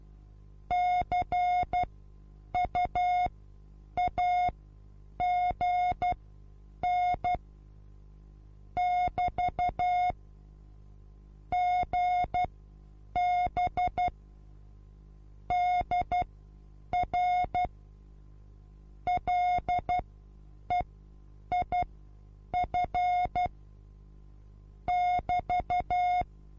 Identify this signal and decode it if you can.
Morse code, heard as a keyed tone, about 12 wpm, spaced as for 8 wpm, 720 Hz. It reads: CUAGN <BT> GB DR LEIF <BT>
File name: Onlinelehrgang / morsen / DJ4UF Morsen23